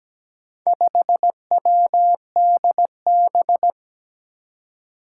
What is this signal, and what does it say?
Morse code, heard as a keyed tone, about 17 wpm, 700 Hz. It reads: 5WDB
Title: Four Challenging Letter-Number Combinations 17wpm